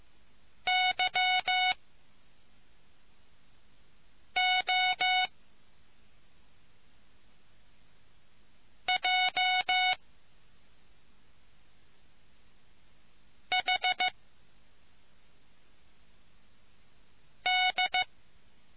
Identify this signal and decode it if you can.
Morse code, heard as a keyed tone, about 15 wpm, spaced as for 3 wpm, 740 Hz. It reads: YOJHD